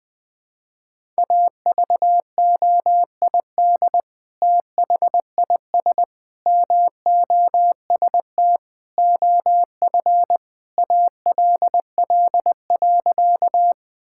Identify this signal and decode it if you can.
Morse code, heard as a keyed tone, about 21 wpm, 700 Hz. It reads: AVOID THIS MOST OF ALL.
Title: Sentences from Top 1000 Words - Review Part 6 - 20wpm